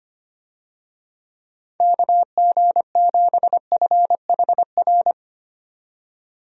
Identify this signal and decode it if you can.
Morse code, heard as a keyed tone, about 25 wpm, 700 Hz. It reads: KG7FHR
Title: USA Call Signs - 6 Characters - 25wpm